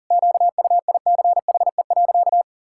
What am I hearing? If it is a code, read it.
Morse code, heard as a keyed tone, about 40 wpm, 700 Hz. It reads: QUICHE.
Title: Pangram Challenge v2 - 40wpm